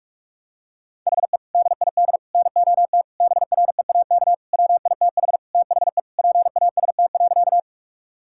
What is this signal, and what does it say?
Morse code, heard as a keyed tone, about 48 wpm, 700 Hz. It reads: HE DID NOT BREAK WITH THE PAST.
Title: Sentences from Top 1000 Words - Review Part 6 - 45wpm